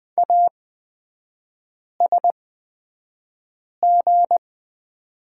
Morse code, heard as a keyed tone, about 20 wpm, spaced as for 5 wpm, 690 Hz. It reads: ASG
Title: 229 Lesson 37e - Letter x - Three Characters 20wpm